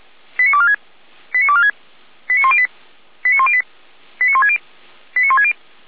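Five-tone selective call (ZVEI), heard as quick runs of five tones, tones of about 80 ms. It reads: 89246 89246 89108 89108 89160 89160
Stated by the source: ZVEI five-tone selective call 89078, received by SDR.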